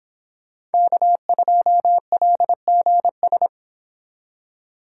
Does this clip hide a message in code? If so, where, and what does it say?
Yes: Morse code from 0.7 to 3.5 s, heard as a keyed tone, about 29 wpm, 700 Hz. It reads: K2LGS